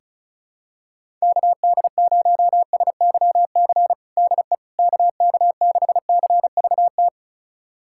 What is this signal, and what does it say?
Morse code, heard as a keyed tone, about 34 wpm, 700 Hz. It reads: KD0SYC DE KK6CVT
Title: USA Callsign DE Callsign 35wpm